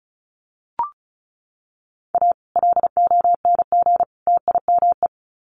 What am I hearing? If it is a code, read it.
Morse code, heard as a keyed tone, about 34 wpm, 700 Hz. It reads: A LONG TIME
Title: Sets of 3 Words 35wpm